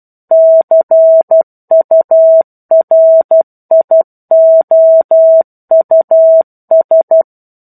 Morse code, heard as a keyed tone, about 12 wpm, 650 Hz. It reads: CURIOUS
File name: 01-Chapter-01 12 WPM